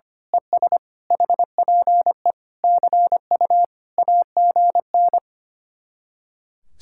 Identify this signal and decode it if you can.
Morse code, heard as a keyed tone, about 24 wpm, 700 Hz. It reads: ES HPE CU AGN
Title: Speed-Racing Common QSO Phrases - 25wpm